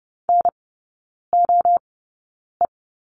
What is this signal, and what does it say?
Morse code, heard as a keyed tone, about 30 wpm, spaced as for 9 wpm, 700 Hz. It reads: NOE
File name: 081 Lesson 14e - Number 2 - Three Characters 30wpm